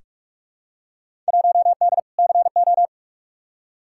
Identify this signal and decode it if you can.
Morse code, heard as a keyed tone, about 46 wpm, 700 Hz. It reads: 1D CO
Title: Field Day Exchanges - 45wpm